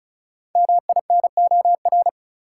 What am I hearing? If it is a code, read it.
Morse code, heard as a keyed tone, about 34 wpm, 700 Hz. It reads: MINOR
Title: Sentences from Top 1500 Words - New Word Drill Alt - 35wpm